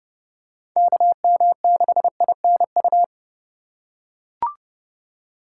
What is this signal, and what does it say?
Morse code, heard as a keyed tone, about 30 wpm, 700 Hz. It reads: KM6INU
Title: USA Call Signs - 3x Character Spacing - 30wpm